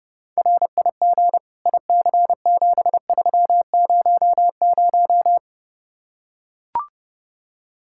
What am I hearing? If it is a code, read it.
Morse code, heard as a keyed tone, about 32 wpm, 700 Hz. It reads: RIG IC7300